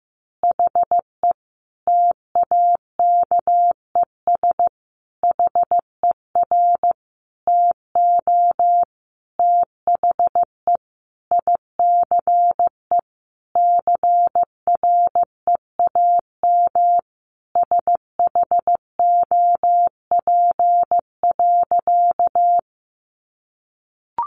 Morse code, heard as a keyed tone, about 15 wpm, 700 Hz. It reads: HE TAKES HER TO THE ICE CREAM SHOP.
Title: Additional Easy Phrases 15wpm